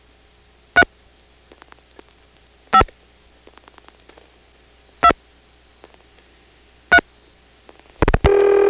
DTMF, heard as telephone touch tones, tones of about 70 ms. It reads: 3 5 2 3
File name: phonetrips / PrinceWmCoVA5HQ